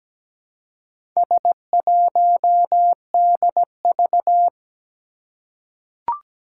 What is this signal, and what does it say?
Morse code, heard as a keyed tone, about 17 wpm, 700 Hz. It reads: S1DV
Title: Four Challenging Letter-Number Combinations 17wpm